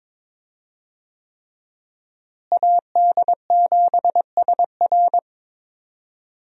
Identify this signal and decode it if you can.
Morse code, heard as a keyed tone, about 22 wpm, 700 Hz. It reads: AD7SR
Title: USA Call Signs - 5 Characters - Repeated 3x - 22wpm